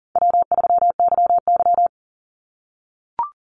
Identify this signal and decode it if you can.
Morse code, heard as a keyed tone, about 40 wpm, 700 Hz. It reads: W3YY
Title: Easy Call Signs - 40wpm